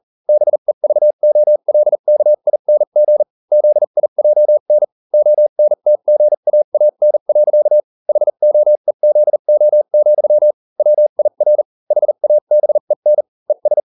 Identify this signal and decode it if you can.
Morse code, heard as a keyed tone, about 40 wpm, spaced as for 35 wpm, 600 Hz. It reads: BEVOLKING ZIJN ONTGAAN. HOEZO, WIR HABEN ES